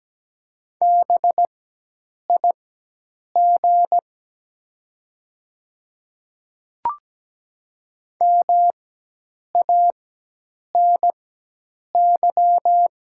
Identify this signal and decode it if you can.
Morse code, heard as a keyed tone, about 17 wpm, spaced as for 8 wpm, 700 Hz. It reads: BIG MANY
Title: 4x Character Spacing - Top 200 Words - 17wpm